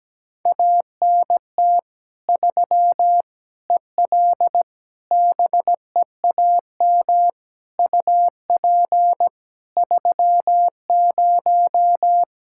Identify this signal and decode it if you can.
Morse code, heard as a keyed tone, about 17 wpm, 700 Hz. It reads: ANT 3 EL BEAM UP 30
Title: Practice QSO - Exchange 2 - Info - 17wpm